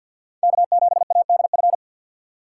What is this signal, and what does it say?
Morse code, heard as a keyed tone, about 53 wpm, 700 Hz. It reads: K7ADF
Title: USA Call Signs - 6x Character Spacing - 50wpm